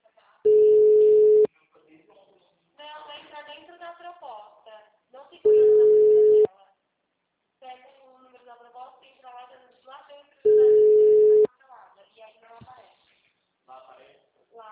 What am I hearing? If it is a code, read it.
Ringback tone (Europe, 425 Hz), with speech.
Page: Araras - SP